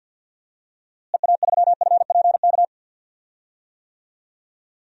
Morse code, heard as a keyed tone, about 53 wpm, 700 Hz. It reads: EA3FPK
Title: DXCC-100 Call Signs - 6 Characters - Repeated 2x - 50wpm